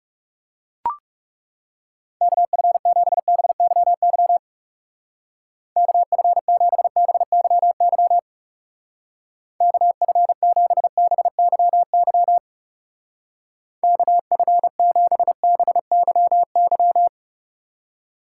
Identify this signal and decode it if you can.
Morse code, heard as a keyed tone, about 37 wpm, 700 Hz. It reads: KF7BYY KF7BYY KF7BYY KF7BYY
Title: Speed Racing - USA 2x3 Call Signs 30wpm